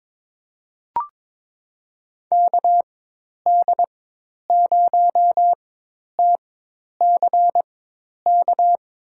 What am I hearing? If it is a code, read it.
Morse code, heard as a keyed tone, about 21 wpm, spaced as for 10 wpm, 700 Hz. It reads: KD0TCK
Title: USA Call Signs - 6 Characters - 4x Character Spacing - 22wpm